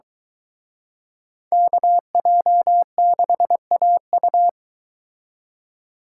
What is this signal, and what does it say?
Morse code, heard as a keyed tone, about 24 wpm, 700 Hz. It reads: KJ6AU